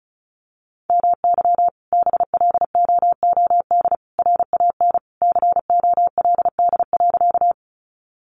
Morse code, heard as a keyed tone, about 34 wpm, 700 Hz. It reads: MY BLOOD RAN COLD.
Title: Sentences from Top 1100 Words 35wpm